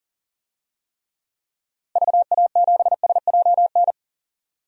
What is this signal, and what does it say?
Morse code, heard as a keyed tone, about 40 wpm, 700 Hz. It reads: VA7SJN
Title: DXCC-100 Call Signs - 6 Characters - Repeated 3x - 40wpm